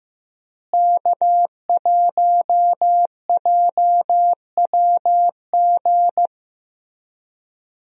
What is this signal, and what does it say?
Morse code, heard as a keyed tone, about 16 wpm, 700 Hz. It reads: K1JWG
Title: USA Call Signs - 5 Characters - Repeated 3x - 15wpm